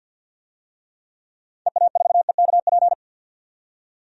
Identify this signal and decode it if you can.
Morse code, heard as a keyed tone, about 52 wpm, 700 Hz. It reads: EA4EKP
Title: DXCC-100 Call Signs - 6 Characters - 50wpm